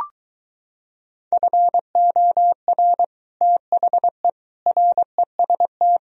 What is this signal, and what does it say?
Morse code, heard as a keyed tone, about 23 wpm, 700 Hz. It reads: FOR THE REST